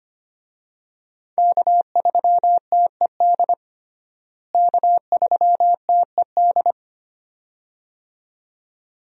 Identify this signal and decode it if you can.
Morse code, heard as a keyed tone, about 25 wpm, 700 Hz. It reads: K3TED K3TED
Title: USA Call Signs - 5 Characters - Repeated 2x - 25wpm